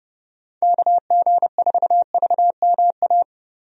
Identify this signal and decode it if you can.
Morse code, heard as a keyed tone, about 30 wpm, 700 Hz. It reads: KG4VMA
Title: USA Call Signs - 6 Characters - 30wpm